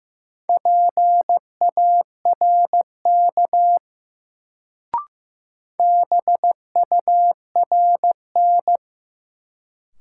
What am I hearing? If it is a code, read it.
Morse code, heard as a keyed tone, about 15 wpm, 700 Hz. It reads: PARK BURN
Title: Sentences from Top 1300 Words - New Word Drill Alt - 15wpm